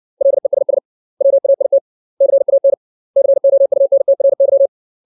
Morse code, heard as a keyed tone, about 60 wpm, 550 Hz. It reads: LESS WAIT CAN COUNTRY
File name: 632 words / 632 Words 60WPM